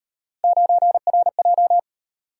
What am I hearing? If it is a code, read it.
Morse code, heard as a keyed tone, about 38 wpm, 700 Hz. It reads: 9RJ